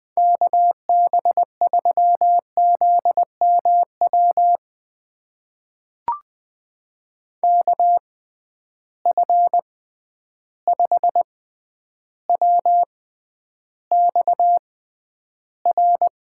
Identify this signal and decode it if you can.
Morse code, heard as a keyed tone, about 20 wpm, spaced as for 10 wpm, 700 Hz. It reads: KB3ZMW KF5WXR
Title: USA Call Signs - 6 Characters - 6x Character Spacing - 20wpm